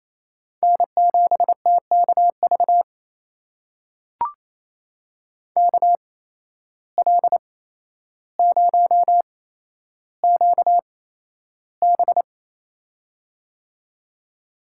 Morse code, heard as a keyed tone, about 28 wpm, spaced as for 12 wpm, 700 Hz. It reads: N7TKV KL0QB
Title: USA Call Signs - 5 Characters - 8x Character Spacing - 28wpm